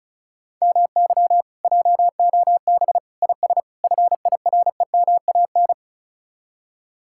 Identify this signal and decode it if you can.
Morse code, heard as a keyed tone, about 43 wpm, 700 Hz. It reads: MY JOB IS FIREMAN
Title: Speed-Racing Common QSO Phrases - 35wpm